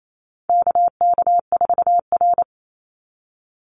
Morse code, heard as a keyed tone, about 28 wpm, 700 Hz. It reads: KK4R
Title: Call Signs - 28wpm